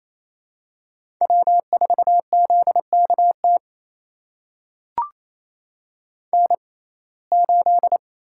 Morse code, heard as a keyed tone, about 28 wpm, spaced as for 20 wpm, 700 Hz. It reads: W4ZKT N8
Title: USA Call Signs - 5 Characters - 6x Character Spacing - 28wpm